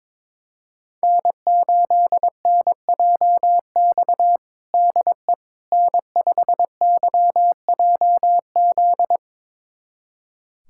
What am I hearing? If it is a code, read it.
Morse code, heard as a keyed tone, about 22 wpm, 700 Hz. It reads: N8NJX DE N5YJZ
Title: USA Callsign DE Callsign - 5 Characters - 22wpm